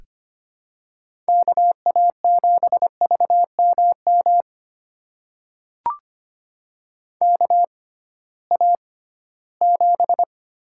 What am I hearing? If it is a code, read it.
Morse code, heard as a keyed tone, about 25 wpm, spaced as for 16 wpm, 700 Hz. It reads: KA7VMM KA7
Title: USA Call Signs - 6x Character Spacing - 25wpm